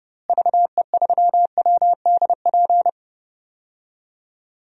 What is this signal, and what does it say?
Morse code, heard as a keyed tone, about 30 wpm, 700 Hz. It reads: VE3WDP
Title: DXCC-100 Call Signs - 6 Characters - Repeated 3x - 30wpm